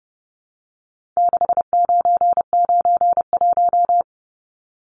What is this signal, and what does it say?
Morse code, heard as a keyed tone, about 30 wpm, 700 Hz. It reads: 6991
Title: Numbers - 4-Digit 30wpm